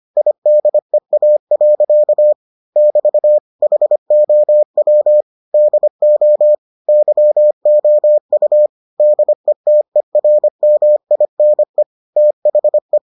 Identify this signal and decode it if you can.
Morse code, heard as a keyed tone, about 25 wpm, 600 Hz. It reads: IDEA. <BT> HOW DO YOU DETERMINE THE